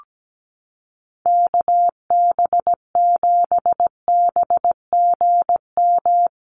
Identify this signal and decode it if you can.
Morse code, heard as a keyed tone, about 17 wpm, 700 Hz. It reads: KB7BGM
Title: USA Call Signs Repeated 3x 17wpm